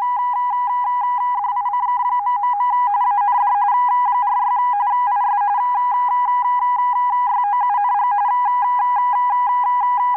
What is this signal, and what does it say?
RTTY, about 45.45 baud. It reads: DICK... AND 46 IS YOUNG